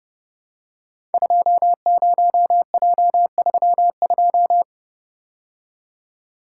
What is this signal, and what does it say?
Morse code, heard as a keyed tone, about 29 wpm, 700 Hz. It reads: 20J32
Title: Code Groups - 5 Letter:Numbers - 30wpm